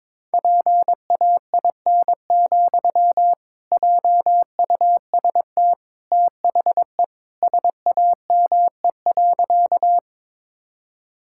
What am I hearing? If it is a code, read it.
Morse code, heard as a keyed tone, about 21 wpm, 700 Hz. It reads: PAIN, JUST THE SAME.
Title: Sentences from Top 1600 Words 22wpm